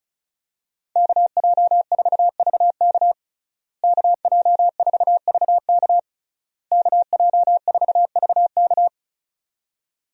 Morse code, heard as a keyed tone, about 34 wpm, 700 Hz. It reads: KJ4VK KJ4VK KJ4VK
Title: USA Call Signs - 5 Characters - Repeated 3x - 35wpm